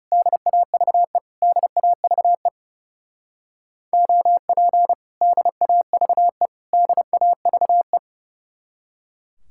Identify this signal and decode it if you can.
Morse code, heard as a keyed tone, about 31 wpm, 700 Hz. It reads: DAVE DAVE OP DAVE DAVE
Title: Speed-Racing Common QSO Phrases - 30wpm